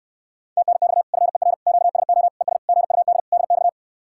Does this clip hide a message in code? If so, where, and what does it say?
Yes: Morse code from 0.6 to 3.7 s, heard as a keyed tone, about 69 wpm, 700 Hz. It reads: TNX FER QSO ES GUD DX